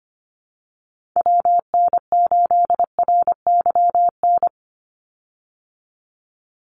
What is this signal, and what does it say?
Morse code, heard as a keyed tone, about 25 wpm, 700 Hz. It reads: WN8RYN
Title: USA Call Signs 25wpm